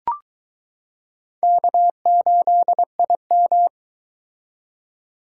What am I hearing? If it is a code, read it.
Morse code, heard as a keyed tone, about 22 wpm, 700 Hz. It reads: K8IM